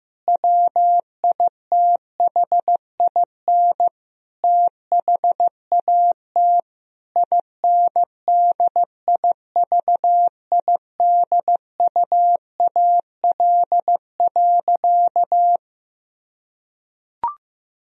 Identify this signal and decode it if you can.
Morse code, heard as a keyed tone, about 15 wpm, 700 Hz. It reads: WITHIN THAT INDIVIDUAL.
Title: Sentences from Top 1000 Words - Review Part 4 - 15wpm